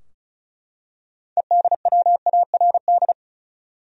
Morse code, heard as a keyed tone, about 36 wpm, 700 Hz. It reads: EDWARD
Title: USA Common Names - 35wpm